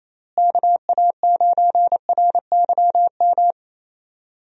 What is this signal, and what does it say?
Morse code, heard as a keyed tone, about 28 wpm, 700 Hz. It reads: KA9RYM
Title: USA Call Signs - 6 Characters - Repeated 3x - 28wpm